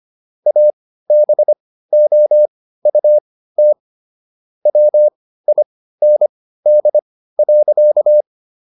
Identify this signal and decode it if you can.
Morse code, heard as a keyed tone, about 25 wpm, spaced as for 15 wpm, 600 Hz. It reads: ABOUT WIND.